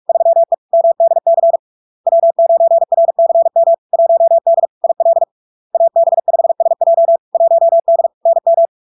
Morse code, heard as a keyed tone, about 46 wpm, 670 Hz. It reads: 3E MDC W9RCM 1D IL AB5SJ 1D NM
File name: LARK-FD-2022-Log-in-CW-45wpm-670-Hz